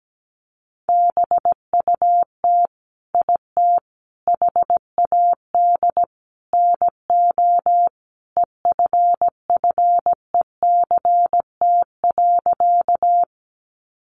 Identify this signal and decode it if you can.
Morse code, heard as a keyed tone, about 17 wpm, 700 Hz. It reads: BUT IT HAD NO EFFECT.
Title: Sentences from Top 1000 Words 17wpm